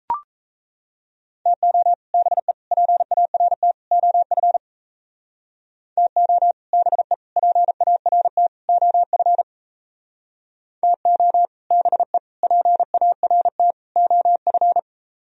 Morse code, heard as a keyed tone, about 35 wpm, 700 Hz. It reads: TO BE PART OF TO BE PART OF TO BE PART OF